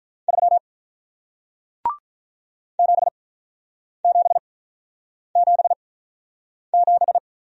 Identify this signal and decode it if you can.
Morse code, heard as a keyed tone, about 47 wpm, spaced as for 9 wpm, 700 Hz. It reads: 3 7777